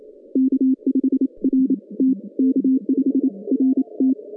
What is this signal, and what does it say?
Morse code, heard as a keyed tone, about 28 wpm, spaced as for 24 wpm, 260 Hz. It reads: K5RT K5RT
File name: 2015 cq ww cw / 2015 cqwww k5tr - 0-27